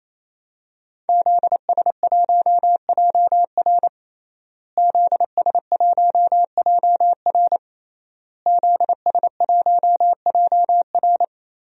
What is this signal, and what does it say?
Morse code, heard as a keyed tone, about 27 wpm, 700 Hz. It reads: ZS1JR ZS1JR ZS1JR
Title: DXCC-100 Call Signs - 5 Characters - Repeated 3x - 28wpm